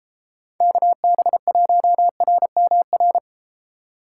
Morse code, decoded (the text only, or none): KB1RMR